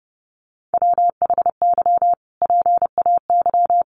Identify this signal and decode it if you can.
Morse code, heard as a keyed tone, about 28 wpm, 700 Hz. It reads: WHY PAY
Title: Sentences from Top 700 Words 30wpm